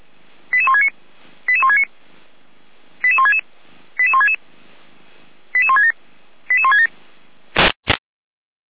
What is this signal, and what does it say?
Five-tone selective call (ZVEI), heard as quick runs of five tones, tones of about 70 ms. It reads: 88169 88169 88166 88166 88167 88167